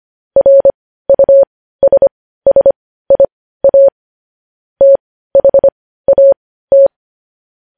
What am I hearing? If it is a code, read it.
Morse code, heard as a keyed tone, about 25 wpm, spaced as for 15 wpm, 550 Hz. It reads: RUSSIA THAT
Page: News Headlines in Morse Code at 15 WPM